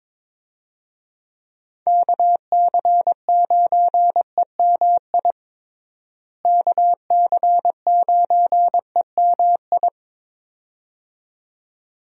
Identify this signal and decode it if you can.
Morse code, heard as a keyed tone, about 22 wpm, 700 Hz. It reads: KC9EMI KC9EMI
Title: USA Call Signs - 6 Characters - Repeated 2x - 22wpm